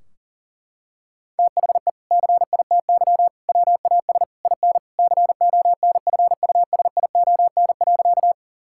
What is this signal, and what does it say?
Morse code, heard as a keyed tone, about 40 wpm, 700 Hz. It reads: THE CITY WAS IN CONFUSION.
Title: Sentences from Top 1600 Words 40wpm